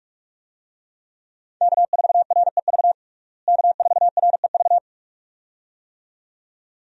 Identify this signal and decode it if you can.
Morse code, heard as a keyed tone, about 46 wpm, 700 Hz. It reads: K4REV K4REV
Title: USA Call Signs - 5 Characters - Repeated 2x - 45wpm